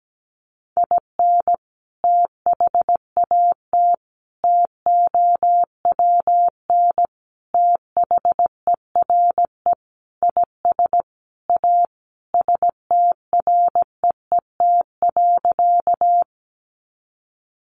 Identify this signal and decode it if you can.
Morse code, heard as a keyed tone, about 17 wpm, 700 Hz. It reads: IN THAT TOWN THERE IS A STREET.
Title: Sentences from Top 500 Words 17wpm